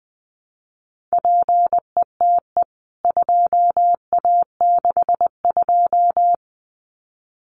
Morse code, heard as a keyed tone, about 20 wpm, 700 Hz. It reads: PETE 2A62
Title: CWT Exchange 20wpm